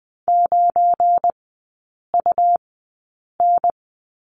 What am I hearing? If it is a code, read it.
Morse code, heard as a keyed tone, about 20 wpm, spaced as for 8 wpm, 700 Hz. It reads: 9UN
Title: 191 Lesson 31e - Letter v - Three Characters 20wpm